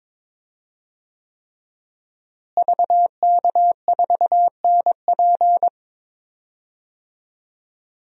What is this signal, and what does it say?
Morse code, heard as a keyed tone, about 22 wpm, 700 Hz. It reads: VK4NP
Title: DXCC-100 Call Signs - 5 Characters - Repeated 3x - 22wpm